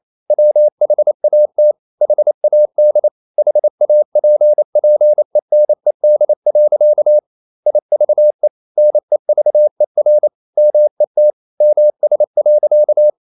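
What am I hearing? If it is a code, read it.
Morse code, heard as a keyed tone, about 28 wpm, 600 Hz. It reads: WHAT HAD HAPPENED. IVE NEVER MET MS.
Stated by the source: Morse code practice at 28 wpm, text of Ken Follett's novel Never in English, from CW med Gnister.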